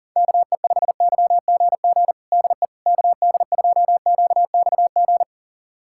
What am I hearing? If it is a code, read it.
Morse code, heard as a keyed tone, about 40 wpm, 700 Hz. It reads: KE5YGG DE KD2QXG
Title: USA Callsign DE Callsign - 6 Characters - 40wpm